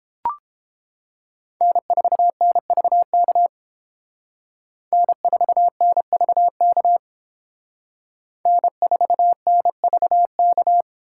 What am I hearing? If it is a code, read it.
Morse code, heard as a keyed tone, about 30 wpm, 700 Hz. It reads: N4NVK N4NVK N4NVK